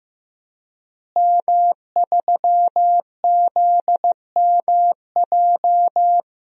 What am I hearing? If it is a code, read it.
Morse code, heard as a keyed tone, about 15 wpm, 700 Hz. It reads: M3ZMJ
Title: DXCC-100 Call Signs - 5 Characters - Repeated 3x - 15wpm